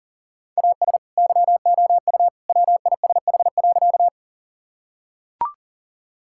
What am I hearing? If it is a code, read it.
Morse code, heard as a keyed tone, about 37 wpm, 700 Hz. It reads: AS YOU WISH.
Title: Sentences from Top 1600 Words 40wpm